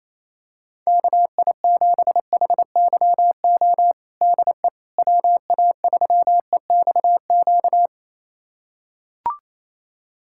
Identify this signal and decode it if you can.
Morse code, heard as a keyed tone, about 28 wpm, 700 Hz. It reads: KI7HYO DE WA3EXQ
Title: USA Callsign DE Callsign - 6 Characters - 28wpm